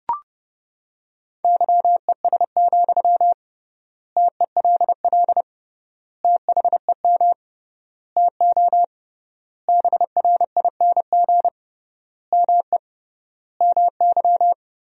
Morse code, heard as a keyed tone, about 32 wpm, 700 Hz. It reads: YES, TELL THEM TO BRING ME MY
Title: 3x Word-Spacing - Sentences from Top 500 Words - 30wpm